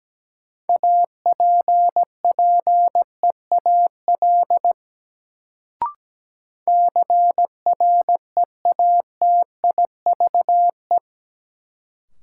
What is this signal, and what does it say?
Morse code, heard as a keyed tone, about 17 wpm, 700 Hz. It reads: APPEAL CREATIVE
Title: Sentences from Top 1300 Words - New Word Drill Alt - 17wpm